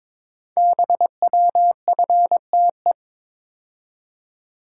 Morse code, heard as a keyed tone, about 22 wpm, 700 Hz. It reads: BWFTE
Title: Code Groups - 5 Letters - 22wpm